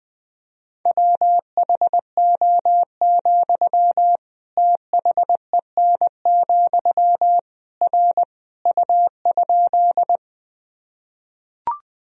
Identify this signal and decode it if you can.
Morse code, heard as a keyed tone, about 20 wpm, 700 Hz. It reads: WHO, THEN, R U?
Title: Sentences from Top 100 Words and Abbreviations 20wpm